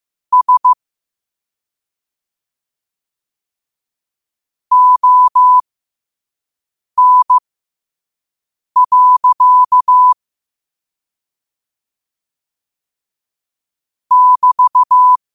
Morse code, heard as a keyed tone, about 15 wpm, spaced as for 5 wpm, 1000 Hz. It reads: S ON. <BT>
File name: qotdfiles / 5wpm